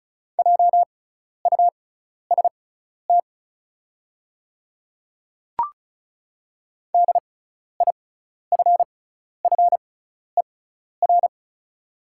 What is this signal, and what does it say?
Morse code, heard as a keyed tone, about 36 wpm, spaced as for 12 wpm, 700 Hz. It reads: JUST DIFFER
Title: 6x Character Spacing - Top 200 Words - 35wpm